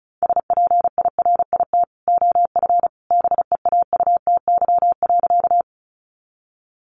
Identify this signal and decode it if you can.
Morse code, heard as a keyed tone, about 34 wpm, 700 Hz. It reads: SPIRIT OF BEAUTY.
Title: Sentences from Top 1400 Words 35wpm